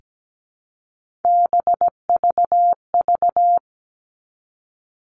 Morse code, heard as a keyed tone, about 17 wpm, 700 Hz. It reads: BVV
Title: All Cut Numbers - 3-Digit 17wpm